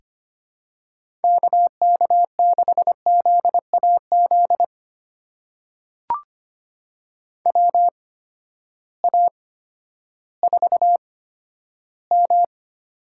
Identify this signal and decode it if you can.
Morse code, heard as a keyed tone, about 25 wpm, spaced as for 12 wpm, 700 Hz. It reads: KK6ZAZ WA4M
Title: USA Call Signs - 6 Characters - 8x Character Spacing - 25wpm